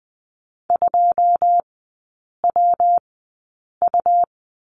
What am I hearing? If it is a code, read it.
Morse code, heard as a keyed tone, about 20 wpm, spaced as for 8 wpm, 700 Hz. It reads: 2WU
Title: 114 Lesson 19e - Letter w - Three Characters 20wpm